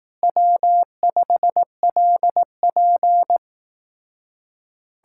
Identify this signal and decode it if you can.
Morse code, heard as a keyed tone, about 18 wpm, 700 Hz. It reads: W5LP
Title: Speed-Racing Common QSO Phrases Encore - 15wpm